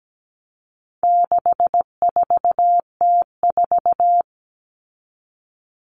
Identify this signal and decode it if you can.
Morse code, heard as a keyed tone, about 17 wpm, 700 Hz. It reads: 64T4
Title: All Cut Numbers - 4-Digit 17wpm